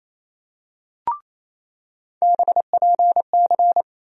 Morse code, heard as a keyed tone, about 28 wpm, 700 Hz. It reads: BPC
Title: USA Call Sign Postfixes - 3 Letters 28wpm